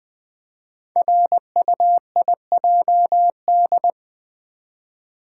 Morse code, heard as a keyed tone, about 20 wpm, 700 Hz. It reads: RUIJD
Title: Code Groups - 5 Letters - 20wpm